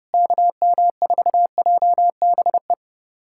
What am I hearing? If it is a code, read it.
Morse code, heard as a keyed tone, about 30 wpm, 700 Hz. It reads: KM4JBE
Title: USA Callsign DE Callsign - 6 Characters - 30wpm